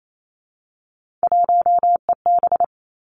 Morse code, heard as a keyed tone, about 28 wpm, 700 Hz. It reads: 1EB
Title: Three Challenging Letter-Number Combinations 28wpm